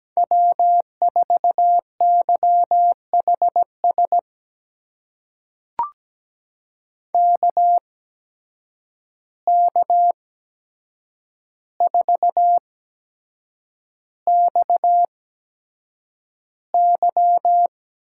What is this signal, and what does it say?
Morse code, heard as a keyed tone, about 17 wpm, spaced as for 7 wpm, 700 Hz. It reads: W4YHS KK4XY
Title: USA Call Signs - 8x Character Spacing - 17wpm